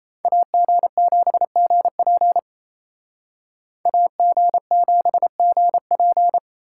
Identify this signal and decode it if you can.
Morse code, heard as a keyed tone, about 30 wpm, 700 Hz. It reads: AG7GP AG7GP
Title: Speed Racing - USA Call Signs 28wpm